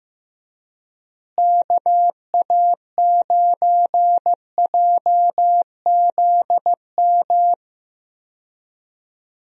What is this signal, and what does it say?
Morse code, heard as a keyed tone, about 16 wpm, 700 Hz. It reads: KA9JZM
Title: USA Call Signs - 6 Characters - Repeated 2x - 15wpm